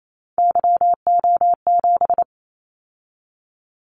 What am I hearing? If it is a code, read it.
Morse code, heard as a keyed tone, about 28 wpm, 700 Hz. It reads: YO7
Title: Callsign Prefixes for Top 25 DXCC - 28wpm